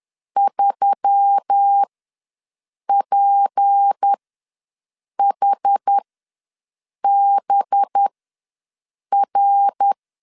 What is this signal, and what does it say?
Morse code, heard as a keyed tone, about 10 wpm, spaced as for 6 wpm, 800 Hz. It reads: 3PHBR
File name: LEZIONI / 9 28 ws 10wpm 4124